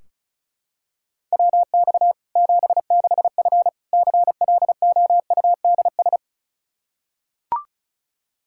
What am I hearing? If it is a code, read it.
Morse code, heard as a keyed tone, about 34 wpm, 700 Hz. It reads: WX 76F CLOUDS
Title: Practice QSO - Exchange 2 - Info - 35wpm